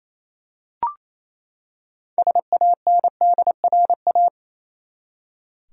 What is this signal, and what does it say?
Morse code, heard as a keyed tone, about 28 wpm, 700 Hz. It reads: SANDRA
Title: USA Common Names - 28wpm